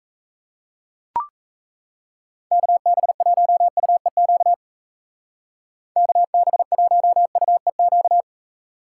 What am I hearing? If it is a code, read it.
Morse code, heard as a keyed tone, about 37 wpm, 700 Hz. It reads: KB1UEQ KB1UEQ